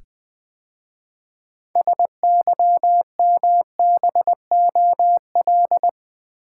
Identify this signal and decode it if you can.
Morse code, heard as a keyed tone, about 20 wpm, 700 Hz. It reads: SYMBOL
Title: 6 Letter Words - First 3 Letters Given - 20wpm